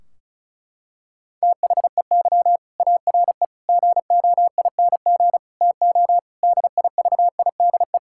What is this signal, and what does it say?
Morse code, heard as a keyed tone, about 34 wpm, 700 Hz. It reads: THEY ARE GOING TO DIVIDE
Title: Sentences from Top 1600 Words 35wpm